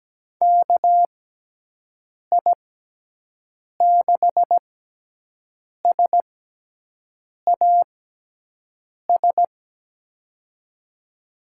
Morse code, heard as a keyed tone, about 17 wpm, spaced as for 6 wpm, 700 Hz. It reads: KI6SAS
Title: USA Call Signs - 6 Characters - 6x Character Spacing - 17wpm